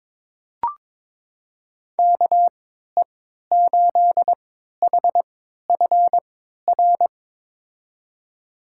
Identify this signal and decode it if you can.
Morse code, heard as a keyed tone, about 22 wpm, spaced as for 13 wpm, 700 Hz. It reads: KE8HFR
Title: USA Call Signs - 6 Characters - 3x Character Spacing - 22wpm